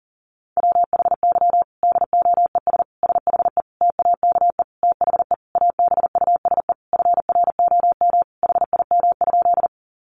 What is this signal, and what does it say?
Morse code, heard as a keyed tone, about 40 wpm, 700 Hz. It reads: WHY DOES SHE TAKE THE ABUSE FROM HIM?
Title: Sentences from Top 1300 Words 40wpm